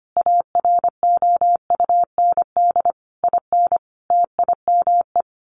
Morse code, heard as a keyed tone, about 25 wpm, 700 Hz. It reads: AROUND IN TIME
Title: A Sound Of Thunder 25wpm